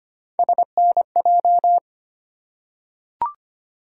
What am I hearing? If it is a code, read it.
Morse code, heard as a keyed tone, about 25 wpm, 700 Hz. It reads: SNJ
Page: morse-code-ninja-practice-25wpm